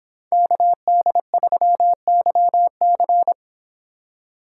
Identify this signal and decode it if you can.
Morse code, heard as a keyed tone, about 25 wpm, 700 Hz. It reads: KD3YC